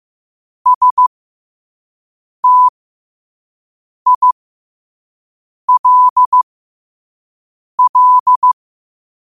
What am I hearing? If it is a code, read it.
Morse code, heard as a keyed tone, about 15 wpm, spaced as for 5 wpm, 1000 Hz. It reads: STILL
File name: qotdfiles / 5wpm